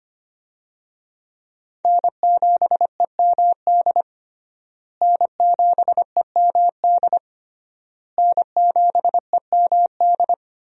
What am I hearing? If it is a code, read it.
Morse code, heard as a keyed tone, about 25 wpm, 700 Hz. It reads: N7EMD N7EMD N7EMD
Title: USA Call Signs - 5 Characters - Repeated 3x - 25wpm